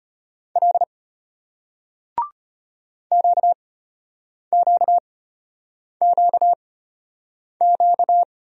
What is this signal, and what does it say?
Morse code, heard as a keyed tone, about 34 wpm, spaced as for 8 wpm, 700 Hz. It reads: L QQQQ